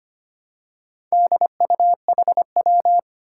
Morse code, heard as a keyed tone, about 25 wpm, 700 Hz. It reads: DUHW
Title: Four Challenging Letter-Number Combinations 25wpm